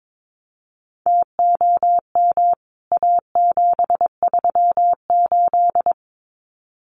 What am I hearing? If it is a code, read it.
Morse code, heard as a keyed tone, about 22 wpm, 700 Hz. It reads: TOM A738
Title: CWT Exchange 22wpm